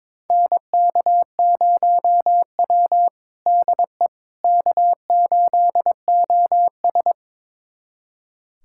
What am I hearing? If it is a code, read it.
Morse code, heard as a keyed tone, about 22 wpm, 700 Hz. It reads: NK0W DE K8OS
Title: USA Callsign DE Callsign - 4 Characters - 22wpm